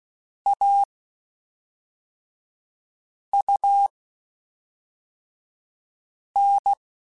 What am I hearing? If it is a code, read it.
Morse code, heard as a keyed tone, about 16 wpm, 800 Hz. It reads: A U N